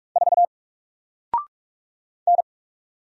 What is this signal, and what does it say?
Morse code, heard as a keyed tone, about 43 wpm, 700 Hz. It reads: 4 N
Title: Single Character (Letter:Number:Punctuation:Prosign) Alt 45wpm